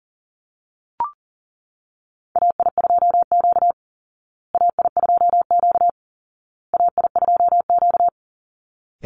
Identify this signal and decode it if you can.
Morse code, heard as a keyed tone, about 40 wpm, 700 Hz. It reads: AI2Q AI2Q AI2Q
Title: Call Signs Repeated 2x - 40wpm